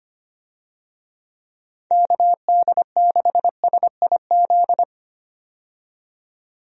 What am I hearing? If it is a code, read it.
Morse code, heard as a keyed tone, about 25 wpm, 700 Hz. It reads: KD6SIZ
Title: USA Call Signs - 6 Characters - 25wpm